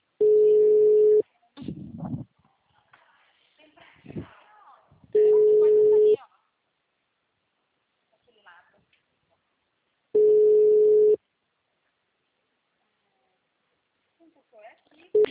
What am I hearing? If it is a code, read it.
Ringback tone (Europe, 425 Hz), with speech.